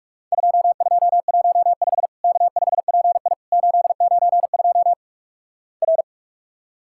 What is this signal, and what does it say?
Morse code, heard as a keyed tone, about 46 wpm, 700 Hz. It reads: 2215 K5PI 892 R
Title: Worked All Europe DX Contest - QTC Exchange - 45wpm